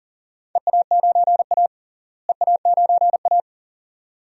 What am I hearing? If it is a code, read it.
Morse code, heard as a keyed tone, about 38 wpm, 700 Hz. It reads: EA9A EA9A
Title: DXCC-100 Call Signs - 4 Characters - Repeated 2x - 40wpm